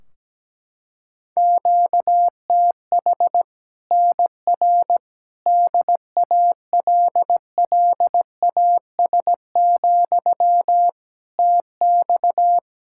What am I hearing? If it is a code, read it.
Morse code, heard as a keyed tone, about 17 wpm, 700 Hz. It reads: QTH NR DALLAS, TX
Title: Practice QSO - Exchange 1 - Beginning - 17wpm